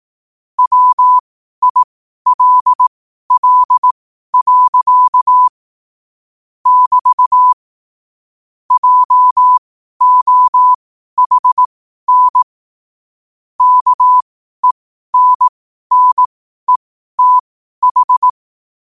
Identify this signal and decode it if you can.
Morse code, heard as a keyed tone, about 18 wpm, spaced as for 12 wpm, 1000 Hz. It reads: WILL. <BT> JOHN KENNETH
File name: qotdfiles / 12wpm